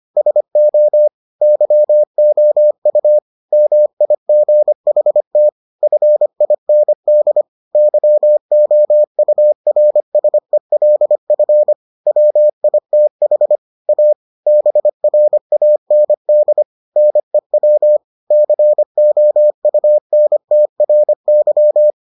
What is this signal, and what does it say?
Morse code, heard as a keyed tone, about 25 wpm, 600 Hz. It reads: SO YOU MIGHT FIND YOURSELF WITH A BRAND NEW COUNTRY